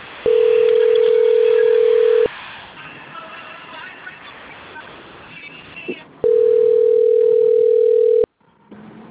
Ringback tone (North America), 440 + 480 Hz.